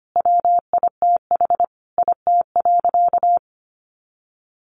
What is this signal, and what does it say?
Morse code, heard as a keyed tone, about 25 wpm, 700 Hz. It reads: WITH IT.
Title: Sentences from Top 1400 Words 25wpm